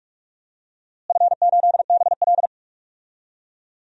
Morse code, heard as a keyed tone, about 42 wpm, 700 Hz. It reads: F8BL
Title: DXCC-100 Call Signs - 4 Characters - Repeated 2x - 45wpm